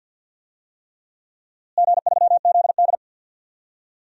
Morse code, heard as a keyed tone, about 52 wpm, 700 Hz. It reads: G3ZD
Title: DXCC-100 Call Signs - 4 Characters - Repeated 3x - 50wpm